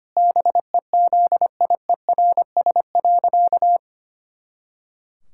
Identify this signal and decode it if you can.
Morse code, heard as a keyed tone, about 25 wpm, 700 Hz. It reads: BEZIERS.
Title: Pangram Challenge v2 - 25wpm